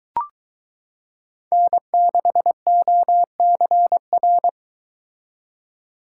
Morse code, heard as a keyed tone, about 22 wpm, 700 Hz. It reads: N6OCR